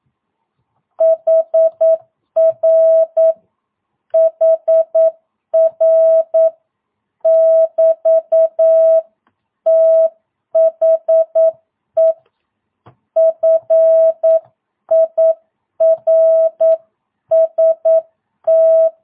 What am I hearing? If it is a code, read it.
Morse code, heard as a keyed tone, about 9 wpm, 650 Hz. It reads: HR HR <BT> THE FIRST
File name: jh8cbh / 2022cw